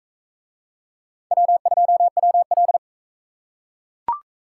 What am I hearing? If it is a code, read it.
Morse code, heard as a keyed tone, about 41 wpm, 700 Hz. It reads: W2WL